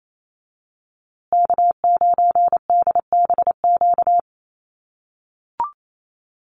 Morse code, heard as a keyed tone, about 28 wpm, 700 Hz. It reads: K9DBQ
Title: USA Call Signs Repeated 3x 28wpm